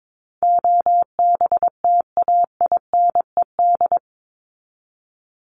Morse code, heard as a keyed tone, about 22 wpm, 700 Hz. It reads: OBTAINED
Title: ED Suffix - Top 100 Words with Base Words 22wpm